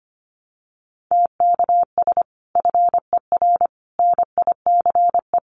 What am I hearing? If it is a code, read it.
Morse code, heard as a keyed tone, about 25 wpm, 700 Hz. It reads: TKS FER NICE
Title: Common QSO Phrases 25wpm